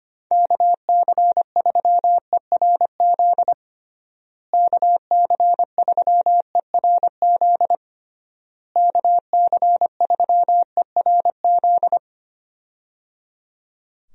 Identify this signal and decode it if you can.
Morse code, heard as a keyed tone, about 25 wpm, 700 Hz. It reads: KC3ERZ KC3ERZ KC3ERZ
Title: USA Call Signs - 6 Characters - Repeated 3x - 25wpm